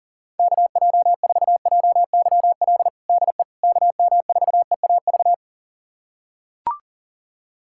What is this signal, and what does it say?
Morse code, heard as a keyed tone, about 40 wpm, 700 Hz. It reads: KJ4JYL DE KM4EAV
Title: USA Callsign DE Callsign 40wpm